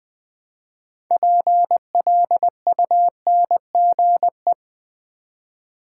Morse code, heard as a keyed tone, about 20 wpm, 700 Hz. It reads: PLUNGE
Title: Build A Word - 3 Letters At A Time - 6-Letter Words - 20wpm